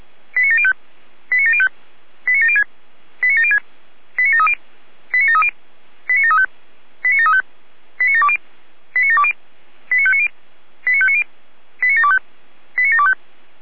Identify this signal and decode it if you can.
Five-tone selective call (ZVEI), heard as quick runs of five tones, tones of about 70 ms. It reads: 89705 89705 89706 89706 89730 89730 89735 89735 89720 89720 89590 89590 89725 89725